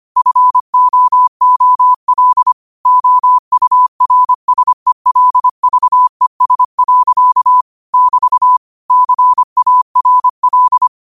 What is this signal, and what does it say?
Morse code, heard as a keyed tone, about 25 wpm, 1000 Hz. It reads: FOOL OURSELVES. <BT> CARL
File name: qotdfiles / 25wpm